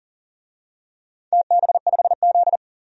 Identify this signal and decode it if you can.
Morse code, heard as a keyed tone, about 40 wpm, 700 Hz. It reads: TB5Z
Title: DXCC-100 Call Signs - 4 Characters - 40wpm